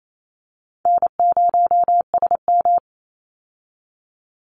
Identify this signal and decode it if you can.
Morse code, heard as a keyed tone, about 28 wpm, 700 Hz. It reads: N0SM
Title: Easy Call Signs - 28wpm